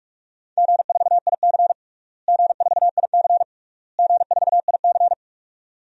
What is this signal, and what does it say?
Morse code, heard as a keyed tone, about 46 wpm, 700 Hz. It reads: G4IC G4IC G4IC
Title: DXCC-100 Call Signs - 4 Characters - Repeated 3x - 45wpm